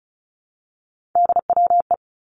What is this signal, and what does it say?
Morse code, heard as a keyed tone, about 36 wpm, 700 Hz. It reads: DWE
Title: Three Challenging Letter-Number Combinations 35wpm